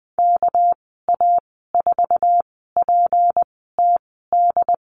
Morse code, heard as a keyed tone, about 20 wpm, spaced as for 15 wpm, 700 Hz. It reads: KA4PTD
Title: 207 Lesson 33g - Letter j - Call Signs 20wpm